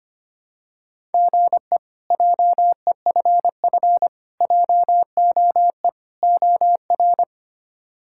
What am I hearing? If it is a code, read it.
Morse code, heard as a keyed tone, about 25 wpm, 700 Hz. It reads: GE JEFF JOE OR
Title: K1USN SST QSO Practice - 25wpm